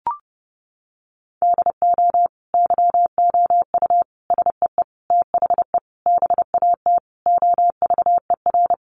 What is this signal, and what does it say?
Morse code, heard as a keyed tone, about 30 wpm, 700 Hz. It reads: DO YOU SEE THE BAT OVER
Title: Sentences from Top 1500 Words 30wpm